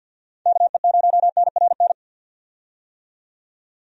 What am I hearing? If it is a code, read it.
Morse code, heard as a keyed tone, about 46 wpm, 700 Hz. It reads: KE0NRN
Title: USA Call Signs - 6 Characters - 50wpm